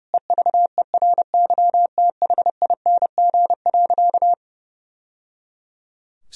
Morse code, heard as a keyed tone, about 30 wpm, 700 Hz. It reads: EVERYTHING.
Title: Sentences from Top 1000 Words - Review Part 1 - 30wpm